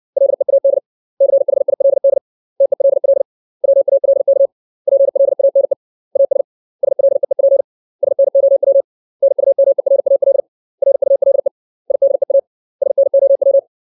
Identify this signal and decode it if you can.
Morse code, heard as a keyed tone, about 60 wpm, 550 Hz. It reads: LEAD CHILD TELL WALK PLANE AS SLEEP SNOW NUMERAL RULE IDEA SNOW
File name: 632 words / 632 Words 60WPM